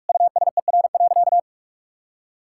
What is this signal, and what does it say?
Morse code, heard as a keyed tone, about 43 wpm, 700 Hz. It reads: USER.
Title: Sentences from Top 1000 Words - Review Part 5 - 45wpm